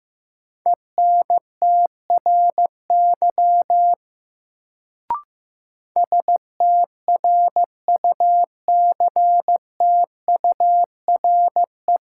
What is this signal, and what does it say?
Morse code, heard as a keyed tone, about 15 wpm, 700 Hz. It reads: ENTRY STRUCTURE
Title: Sentences from Top 1500 Words - New Word Drill Alt - 15wpm